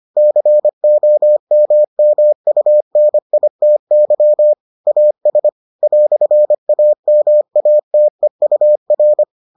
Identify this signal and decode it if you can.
Morse code, heard as a keyed tone, about 25 wpm, 600 Hz. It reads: COMMUNITY AS "AMATEUR